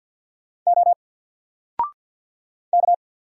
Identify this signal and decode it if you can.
Morse code, heard as a keyed tone, about 46 wpm, 700 Hz. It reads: O K
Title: Single Letter-Number 50wpm